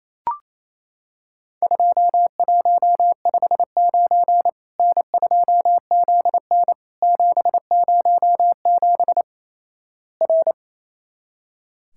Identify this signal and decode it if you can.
Morse code, heard as a keyed tone, about 28 wpm, 700 Hz. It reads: 2159 N2ZN 707 R
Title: Worked All Europe DX Contest - QTC Exchange - 28wpm